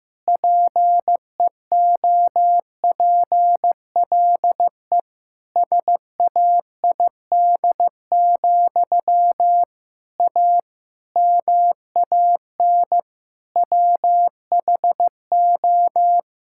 Morse code, heard as a keyed tone, about 15 wpm, 700 Hz. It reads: PEOPLE SAID, A MAN WHO
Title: Sentences from Top 1000 Words - Review Part 6 - 15wpm